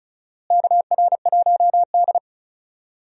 Morse code, heard as a keyed tone, about 36 wpm, 700 Hz. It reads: KR1D
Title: USA Call Signs - 4 Characters - Repeated 3x - 35wpm